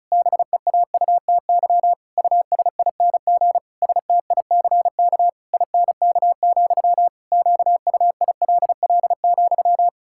Morse code, heard as a keyed tone, about 36 wpm, 700 Hz. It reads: BEAUTY USING STICK INK, QUILL,
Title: Pangram Challenge v2 - 35wpm